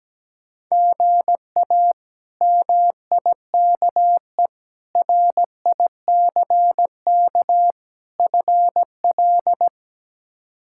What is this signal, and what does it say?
Morse code, heard as a keyed tone, about 17 wpm, 700 Hz. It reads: GA MIKE RICK FL
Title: K1USN SST QSO Practice - 17wpm